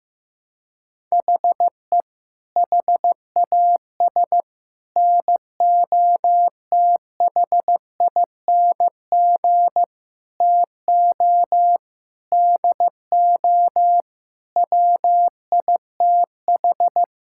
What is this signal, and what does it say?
Morse code, heard as a keyed tone, about 15 wpm, 700 Hz. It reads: HE HAS NOTHING TO DO WITH
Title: Sentences from Top 1000 Words - Review Part 5 - 15wpm